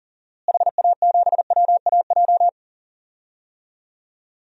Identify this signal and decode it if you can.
Morse code, heard as a keyed tone, about 37 wpm, 700 Hz. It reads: HA7WAJ
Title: DXCC-100 Call Signs - 6 Characters - Repeated 2x - 40wpm